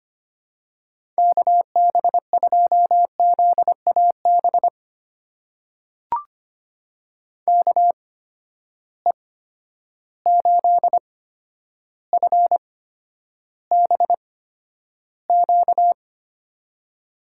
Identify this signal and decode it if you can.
Morse code, heard as a keyed tone, about 25 wpm, spaced as for 11 wpm, 700 Hz. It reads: KB2ZAB KE8FBQ
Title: USA Call Signs - 6 Characters - 8x Character Spacing - 25wpm